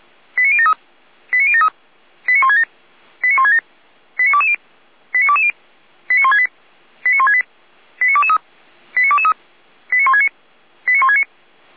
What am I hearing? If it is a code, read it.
Five-tone selective call (ZVEI), heard as quick runs of five tones, tones of about 70 ms. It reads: 89073 89073 89167 89167 89220 89220 89168 89168 89223 89223 89169 89169
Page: audio.f85360.feuerwehr-orlamuende.org